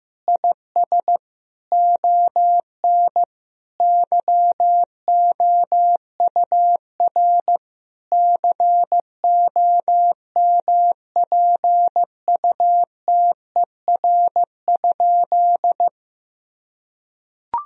Morse code, heard as a keyed tone, about 15 wpm, 700 Hz. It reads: IS ON YOUR COMPUTER?
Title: Sentences from Top 1000 Words - Review Part 3 - 15wpm